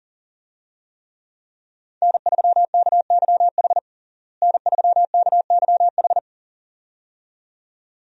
Morse code, heard as a keyed tone, about 40 wpm, 700 Hz. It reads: N3KYH N3KYH
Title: USA Call Signs - 5 Characters - Repeated 2x - 40wpm